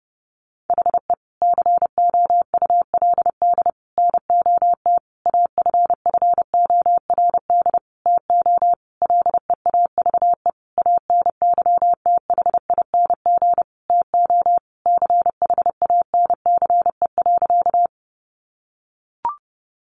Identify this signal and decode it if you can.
Morse code, heard as a keyed tone, about 30 wpm, 700 Hz. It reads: HE COULD NOT AFFORD TO LEAVE ANYTHING TO CHANCE.
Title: Sentences from Top 1100 Words 30wpm